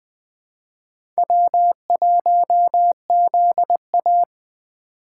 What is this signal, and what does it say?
Morse code, heard as a keyed tone, about 20 wpm, 700 Hz. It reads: W1ZA
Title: USA Call Signs - 3x Character Spacing - 20wpm